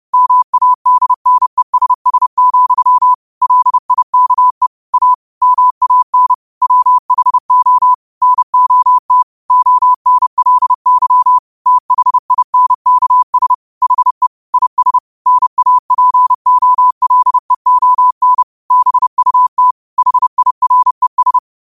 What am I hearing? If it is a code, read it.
Morse code, heard as a keyed tone, about 30 wpm, 1000 Hz. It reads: MADNESS, LIKE A MAN WHO NOT ONLY THINKS HE IS NAPOLEON BUT HIRES
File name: qotdfiles / 30wpm